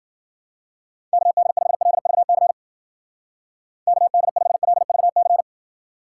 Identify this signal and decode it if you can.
Morse code, heard as a keyed tone, about 56 wpm, 700 Hz. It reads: KD5LVC KD5LVC